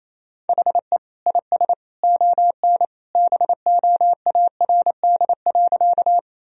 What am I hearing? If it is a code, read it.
Morse code, heard as a keyed tone, about 27 wpm, 700 Hz. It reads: HE IS ON BOARD.
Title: Sentences from Top 1000 Words - Review Part 4 - 28wpm